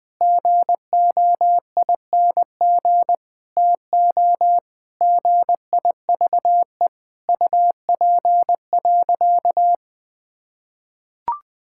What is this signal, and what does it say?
Morse code, heard as a keyed tone, about 21 wpm, 700 Hz. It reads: GOING TO GIVE UP.
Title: Sentences with 5-Word Phrases - 20wpm